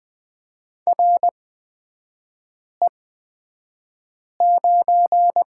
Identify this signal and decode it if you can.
Morse code, heard as a keyed tone, about 20 wpm, spaced as for 5 wpm, 700 Hz. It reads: RE9
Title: 255 Lesson 41e - Period - Three Characters 20wpm